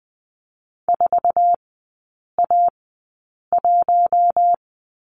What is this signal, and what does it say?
Morse code, heard as a keyed tone, about 20 wpm, spaced as for 8 wpm, 700 Hz. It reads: 4A1
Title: 047 Lesson 9e - Number 4 - Three Characters 20wpm